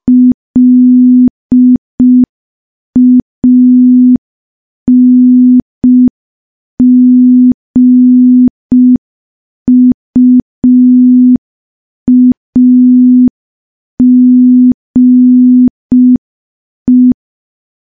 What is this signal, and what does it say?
Morse code, heard as a keyed tone, about 5 wpm, 250 Hz. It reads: LANGUAGE
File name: morse / morse-training-05WPM